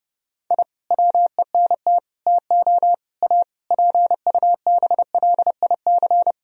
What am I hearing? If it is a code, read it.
Morse code, heard as a keyed tone, about 30 wpm, 700 Hz. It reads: I WENT TO A PUBLIC
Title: Sentences from Top 1000 Words - Review Part 3 - 30wpm